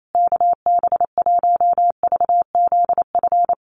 Morse code, heard as a keyed tone, about 28 wpm, 700 Hz. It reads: KB1VZF
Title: USA Call Signs Repeated 2x 28wpm